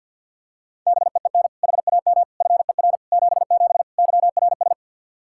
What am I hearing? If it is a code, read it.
Morse code, heard as a keyed tone, about 53 wpm, 700 Hz. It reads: BEEN HAM FER 77 YRS
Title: Practice QSO - Exchange 3 - More Info - 50wpm